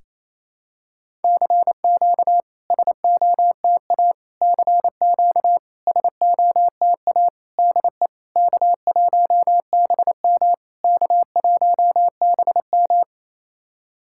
Morse code, heard as a keyed tone, about 28 wpm, 690 Hz. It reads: CQ SOTA CQ SOTA DE K1BM K1BM
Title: SOTA QSO Practice 28wpm